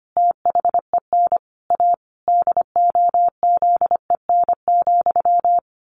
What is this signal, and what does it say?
Morse code, heard as a keyed tone, about 25 wpm, 700 Hz. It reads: THEN A DOZEN,
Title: A Sound Of Thunder 25wpm